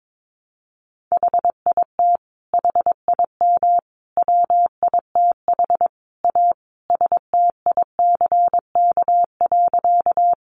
Morse code, heard as a keyed tone, about 22 wpm, 700 Hz. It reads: HIT HIM WITH A STICK.
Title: Sentences from Top 1200 Words 22wpm